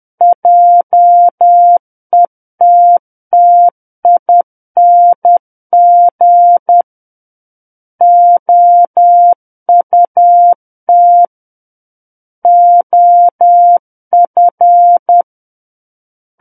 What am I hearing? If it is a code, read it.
Morse code, heard as a keyed tone, about 10 wpm, 700 Hz. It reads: JETTING OUT OF
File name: Book One / 01-Chapter 1 - 10 WPM